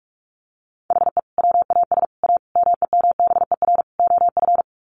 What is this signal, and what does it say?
Morse code, heard as a keyed tone, about 44 wpm, 700 Hz. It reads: HE WAS A MEMBER OF
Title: Sets of 5 Words Encore2 45wpm